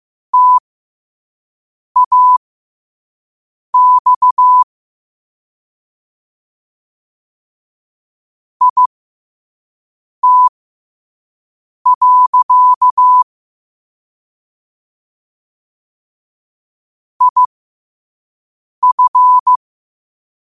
Morse code, heard as a keyed tone, about 15 wpm, spaced as for 5 wpm, 1000 Hz. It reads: TAX IT. IF
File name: qotdfiles / 5wpm